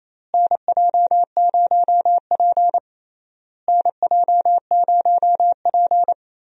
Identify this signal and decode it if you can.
Morse code, heard as a keyed tone, about 28 wpm, 700 Hz. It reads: NJ0P NJ0P
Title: USA Call Signs - 4 Characters - Repeated 3x - 28wpm